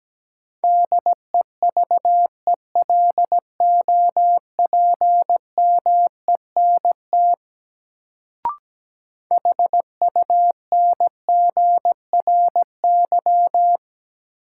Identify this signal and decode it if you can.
Morse code, heard as a keyed tone, about 17 wpm, 700 Hz. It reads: DEVELOPMENT HUNGRY
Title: Sentences from Top 1400 Words - New Word Drill Alt - 17wpm